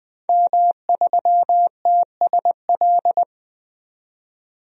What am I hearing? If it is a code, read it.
Morse code, heard as a keyed tone, about 20 wpm, 700 Hz. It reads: M3TSL
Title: DXCC-100 Call Signs - 5 Characters - 20wpm